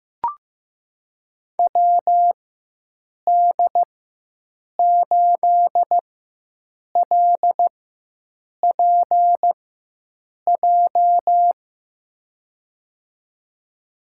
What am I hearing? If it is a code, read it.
Morse code, heard as a keyed tone, about 15 wpm, spaced as for 7 wpm, 700 Hz. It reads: WD8LPJ
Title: USA Call Signs - 6 Characters - 4x Character Spacing - 15wpm